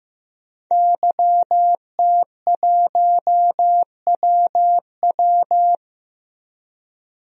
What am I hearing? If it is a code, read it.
Morse code, heard as a keyed tone, about 15 wpm, 700 Hz. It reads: YT1WW
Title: DXCC-100 Call Signs - 5 Characters - 15wpm